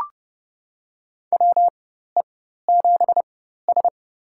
Morse code, heard as a keyed tone, about 30 wpm, spaced as for 14 wpm, 700 Hz. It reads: WE7S
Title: USA Call Signs - 4x Character Spacing - 30wpm